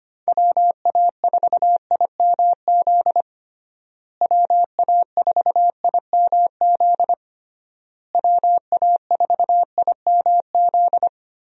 Morse code, heard as a keyed tone, about 25 wpm, 700 Hz. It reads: WA4IMZ WA4IMZ WA4IMZ
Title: USA Call Signs - 6 Characters - Repeated 3x - 25wpm